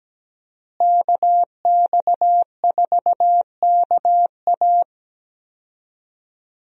Morse code, heard as a keyed tone, about 17 wpm, 700 Hz. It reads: KX4KA
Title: USA Call Signs - 5 Characters - Repeated 2x - 17wpm